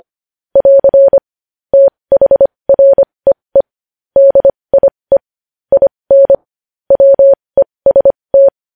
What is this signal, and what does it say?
Morse code, heard as a keyed tone, about 25 wpm, spaced as for 20 wpm, 550 Hz. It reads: <AR> THREE DIE IN WEST
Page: news-headlines-in-morse-code-at-20-wpm